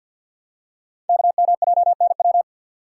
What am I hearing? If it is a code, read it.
Morse code, heard as a keyed tone, about 46 wpm, 700 Hz. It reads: KMJNW
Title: Code Groups - 5 Letters - 50wpm